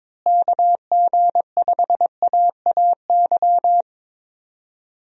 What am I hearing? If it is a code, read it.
Morse code, heard as a keyed tone, about 21 wpm, 700 Hz. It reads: KG5AAY
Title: USA Call Signs - 6 Characters - Repeated 3x - 22wpm